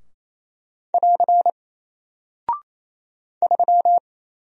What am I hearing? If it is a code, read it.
Morse code, heard as a keyed tone, about 28 wpm, 700 Hz. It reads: <AR> 3
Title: Single Character (Letter:Number:Punctuation:Prosign) Alt 28wpm